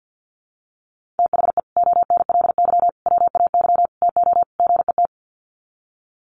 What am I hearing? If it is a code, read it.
Morse code, heard as a keyed tone, about 48 wpm, 700 Hz. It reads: THE ONLY WAY TO GET
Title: Sets of 5 Words Encore2 50wpm